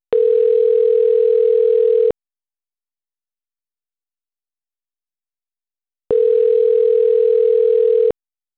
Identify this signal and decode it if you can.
Ringback tone (North America), 440 + 480 Hz.